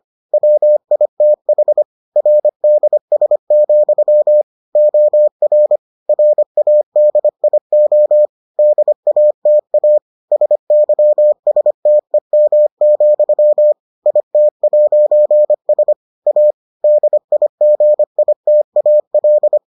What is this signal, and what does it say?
Morse code, heard as a keyed tone, about 25 wpm, 600 Hz. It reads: WITH RDS, OR RADIO DATA SYSTEM, IT'S A DIGITAL